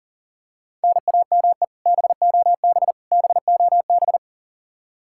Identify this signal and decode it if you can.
Morse code, heard as a keyed tone, about 40 wpm, 700 Hz. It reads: NAME BOB BOB
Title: K1USN SST QSO Practice 40wpm